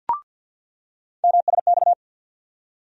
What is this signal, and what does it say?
Morse code, heard as a keyed tone, about 47 wpm, 700 Hz. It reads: MSX
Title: USA Call Sign Postfixes - 3 Letters 50wpm